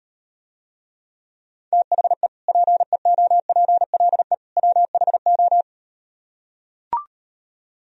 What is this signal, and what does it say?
Morse code, heard as a keyed tone, about 38 wpm, 700 Hz. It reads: THE PEOPLE WHO